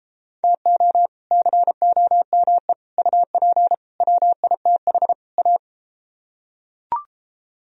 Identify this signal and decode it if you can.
Morse code, heard as a keyed tone, about 32 wpm, 700 Hz. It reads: TO COME UP WITH A